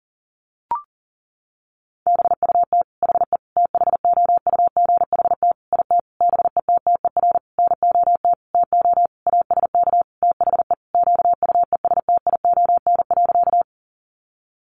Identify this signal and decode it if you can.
Morse code, heard as a keyed tone, about 40 wpm, 700 Hz. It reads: BUT HE THOUGHT IT BETTER NOT TO ASK THE QUESTION.
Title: Sentences from Top 600 Words 40wpm